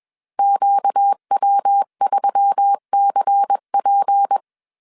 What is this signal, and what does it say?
Morse code, heard as a keyed tone, about 21 wpm, 800 Hz. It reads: QW3CP
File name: CW / DET / 1 6 1500C S C 20wpm 2116